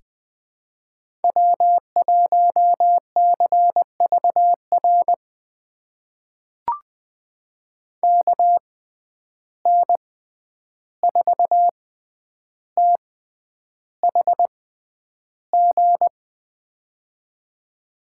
Morse code, heard as a keyed tone, about 20 wpm, spaced as for 10 wpm, 700 Hz. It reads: W1CVR KN4THG
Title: USA Call Signs - 6x Character Spacing - 20wpm